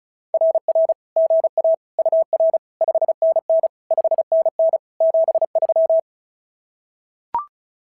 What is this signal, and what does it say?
Morse code, heard as a keyed tone, about 36 wpm, 650 Hz. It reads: RR GA UR 5NN 5NN 73
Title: SOTA QSO Practice 35wpm